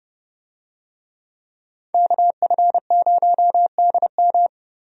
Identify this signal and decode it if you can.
Morse code, heard as a keyed tone, about 29 wpm, 700 Hz. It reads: KF0DM
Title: USA Call Signs - 5 Characters - Repeated 2x - 30wpm